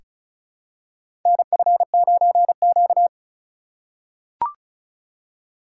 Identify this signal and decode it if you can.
Morse code, heard as a keyed tone, about 36 wpm, 700 Hz. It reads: NF9Q
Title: USA Call Signs - 4x Character Spacing - 35wpm